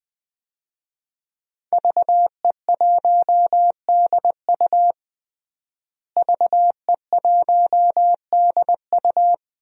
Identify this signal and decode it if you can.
Morse code, heard as a keyed tone, about 20 wpm, 700 Hz. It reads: VE1DU VE1DU
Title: DXCC-100 Call Signs - 5 Characters - Repeated 2x - 20wpm